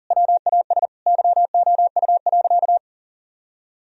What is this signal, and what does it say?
Morse code, heard as a keyed tone, about 40 wpm, 700 Hz. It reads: WAS YOU.
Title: Sentences from Top 1000 Words - Review Part 4 - 40wpm